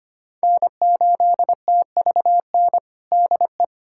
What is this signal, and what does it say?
Morse code, heard as a keyed tone, about 25 wpm, 700 Hz. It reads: N8TVN DE
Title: USA Callsign DE Callsign 25wpm